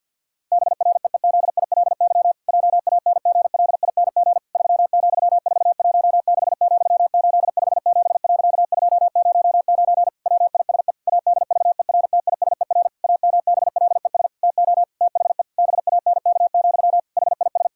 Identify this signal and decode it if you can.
Morse code, heard as a keyed tone, about 50 wpm, 700 Hz. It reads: BREEZILY JANGLING 3,416,857.209 WISE ADVERTISER AMBLES TO THE BANK, HIS
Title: Pangram Challenge v2 - 50wpm